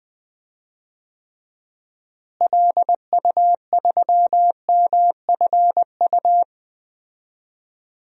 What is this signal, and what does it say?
Morse code, heard as a keyed tone, about 20 wpm, 700 Hz. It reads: LU3MFU
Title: DXCC-100 Call Signs - 6 Characters - Repeated 2x - 20wpm